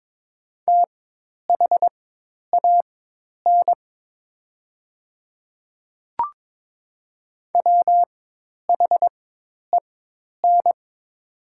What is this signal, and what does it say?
Morse code, heard as a keyed tone, about 22 wpm, spaced as for 10 wpm, 700 Hz. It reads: THAN WHEN
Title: 4x Character Spacing - Top 100 Words - 22wpm